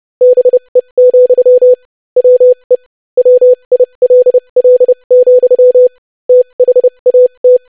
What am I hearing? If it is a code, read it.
Morse code, heard as a keyed tone, about 30 wpm, spaced as for 25 wpm, 500 Hz. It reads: BE, WE WILL, THAT
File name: lesson / basicphrase-25wpm